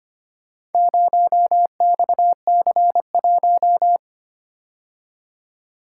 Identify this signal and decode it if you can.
Morse code, heard as a keyed tone, about 25 wpm, 700 Hz. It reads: 0XC1
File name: practice / 25wpm / Four Letter-Number Combinations - 25wpm